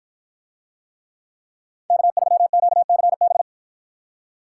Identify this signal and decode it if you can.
Morse code, heard as a keyed tone, about 53 wpm, 700 Hz. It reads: K3QCB